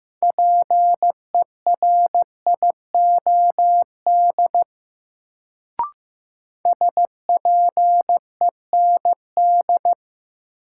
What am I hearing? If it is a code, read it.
Morse code, heard as a keyed tone, about 15 wpm, 700 Hz. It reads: PERIOD SPEND
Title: Sentences from Top 800 Words - New Word Drill - 15wpm